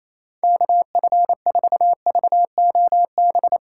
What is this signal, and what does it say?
Morse code, heard as a keyed tone, about 28 wpm, 700 Hz. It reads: KF4VOB
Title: USA Call Signs - 6 Characters - 2x Character Spacing - 28wpm